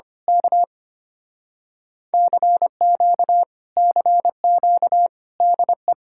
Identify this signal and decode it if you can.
Morse code, heard as a keyed tone, about 24 wpm, 700 Hz. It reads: K CQ CQ DE
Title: Speed-Racing Common QSO Phrases - 25wpm